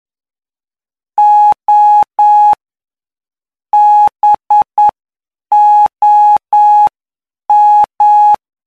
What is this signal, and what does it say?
Morse code, heard as a keyed tone, about 9 wpm, spaced as for 7 wpm, 840 Hz. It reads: O BOM
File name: cw / py2lcd / 19-PY2LCD-Li��o 28